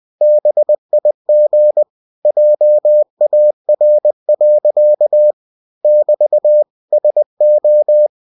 Morse code, heard as a keyed tone, about 20 wpm, 600 Hz. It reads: BIG JAR. <BT> SO
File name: cw-books / winnie-the-pooh / winnie-20wpm-page-0095